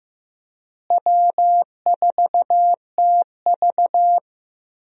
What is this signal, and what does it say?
Morse code, heard as a keyed tone, about 15 wpm, 700 Hz. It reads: W4TV
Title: USA Call Signs - 4 Characters - Repeated 2x - 15wpm